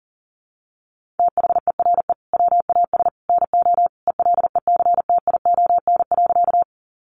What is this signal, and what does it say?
Morse code, heard as a keyed tone, about 37 wpm, 700 Hz. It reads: THERE WAS NO ELECTION.
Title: Sentences from Top 1500 Words 40wpm